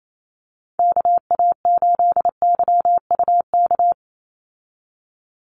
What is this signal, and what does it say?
Morse code, heard as a keyed tone, about 28 wpm, 700 Hz. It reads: KA8YUK
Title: USA Call Signs 28wpm